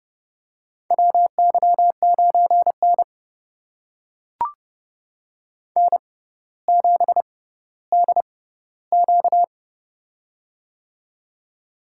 Morse code, heard as a keyed tone, about 31 wpm, spaced as for 15 wpm, 700 Hz. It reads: WY9N N7DQ
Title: USA Call Signs - 4 Characters - 6x Character Spacing - 30wpm